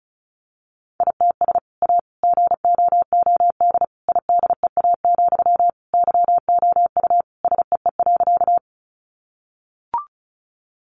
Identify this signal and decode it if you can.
Morse code, heard as a keyed tone, about 34 wpm, 700 Hz. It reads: ITS A GOOD IDEA, YOU SEE.
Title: Sentences from Top 600 Words 35wpm